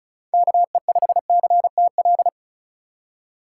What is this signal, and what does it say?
Morse code, heard as a keyed tone, about 34 wpm, 700 Hz. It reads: KE5CTL
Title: USA Call Signs - 6 Characters - 3x Character Spacing - 35wpm